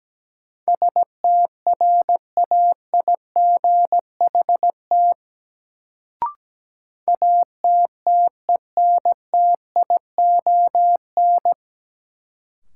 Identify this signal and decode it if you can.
Morse code, heard as a keyed tone, about 17 wpm, 700 Hz. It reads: STRAIGHT ATTENTION
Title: Sentences from Top 1000 Words - New Word Drill - 17wpm